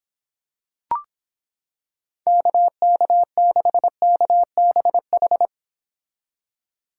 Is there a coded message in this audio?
Yes: Morse code from 2.3 to 5.5 s, heard as a keyed tone, about 26 wpm, 700 Hz. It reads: KK6KBH